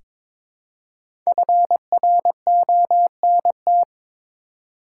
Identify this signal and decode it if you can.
Morse code, heard as a keyed tone, about 22 wpm, 700 Hz. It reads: FRONT
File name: 5 Letter Words - First 3 Letters Given - 22wpm